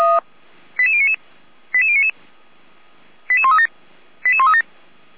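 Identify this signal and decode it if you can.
Five-tone selective call (ZVEI), heard as quick runs of five tones, tones of about 70 ms. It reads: 88099 88099 88137 88137
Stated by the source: ZVEI five-tone selective call received on an SDR.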